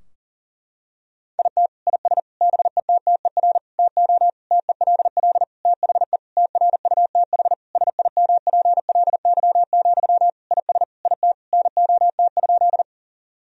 Morse code, heard as a keyed tone, about 37 wpm, 700 Hz. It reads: IT IS BETTER TO TELL THE TRUTH SIMPLY, IS IT NOT?
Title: Sentences from Top 1000 Words - Review Part 4 - 40wpm